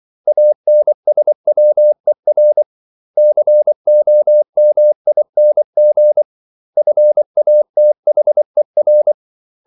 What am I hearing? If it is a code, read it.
Morse code, heard as a keyed tone, about 24 wpm, 600 Hz. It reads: ANSWER COMING FATHER
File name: CW / Practice / 500MostCommonEnglishWords24